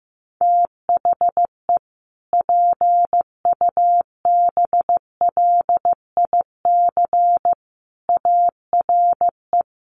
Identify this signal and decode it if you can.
Morse code, heard as a keyed tone, about 14 wpm, 700 Hz. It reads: THE PUBLIC ARE
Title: Sentences from Top 1300 Words 15wpm